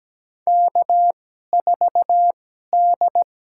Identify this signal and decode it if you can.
Morse code, heard as a keyed tone, about 17 wpm, spaced as for 12 wpm, 700 Hz. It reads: K4D
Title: USA Call Signs - 2x Character Spacing - 17wpm